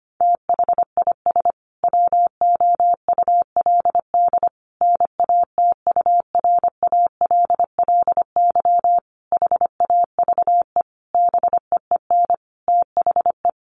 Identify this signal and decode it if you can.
Morse code, heard as a keyed tone, about 25 wpm, 700 Hz. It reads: THIS WOULD NATURALLY HAVE BEEN THE
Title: Sentences from Top 1300 Words 25wpm